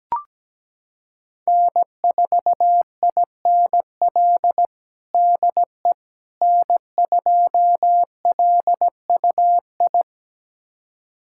Morse code, heard as a keyed tone, about 17 wpm, 700 Hz. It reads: N4INL DE N2LUI
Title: USA Callsign DE Callsign - 5 Characters - 17wpm